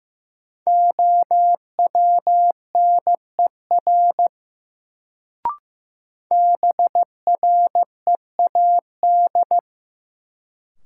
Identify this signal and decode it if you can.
Morse code, heard as a keyed tone, about 15 wpm, 700 Hz. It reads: OWNER BREAD
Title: Sentences from Top 1600 Words - New Word Drill - 15wpm